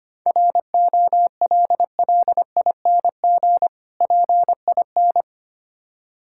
Morse code, heard as a keyed tone, about 25 wpm, 700 Hz. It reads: ROLLING PIN
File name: practice / 25wpm / Difficult Words - Speed-Racing with Hints - 25wpm